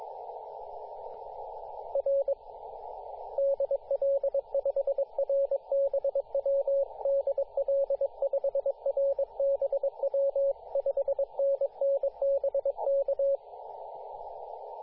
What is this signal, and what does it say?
Morse code, heard as a keyed tone, about 22 wpm, 550 Hz, with radio noise. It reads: R DL5RBW DL5RBW 5NN BK
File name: posts / 2017-sep-22-0800z-jd1bon-14-040mhz-cw